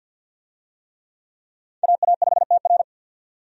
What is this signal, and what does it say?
Morse code, heard as a keyed tone, about 53 wpm, 700 Hz. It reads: AA5TR
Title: USA Call Signs - 5 Characters - 50wpm